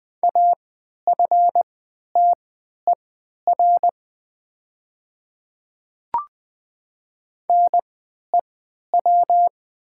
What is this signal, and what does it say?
Morse code, heard as a keyed tone, about 20 wpm, spaced as for 11 wpm, 700 Hz. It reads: AFTER NEW
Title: 3x Character Spacing - Top 200 Words - 20wpm